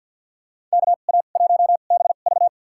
Morse code, heard as a keyed tone, about 47 wpm, spaced as for 35 wpm, 700 Hz. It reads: KA1BV
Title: USA Call Signs - 5 Characters - 2x Character Spacing - 50wpm